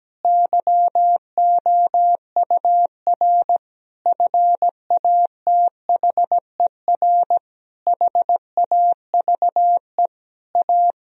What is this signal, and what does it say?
Morse code, heard as a keyed tone, about 17 wpm, 700 Hz. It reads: YOUR FATHER HAVE A
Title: Sentences from Top 1000 Words - Review Part 3 - 17wpm